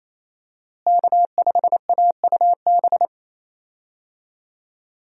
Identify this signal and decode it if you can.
Morse code, heard as a keyed tone, about 28 wpm, 700 Hz. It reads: K5AUB
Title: USA Call Signs - 5 Characters - Repeated 2x - 28wpm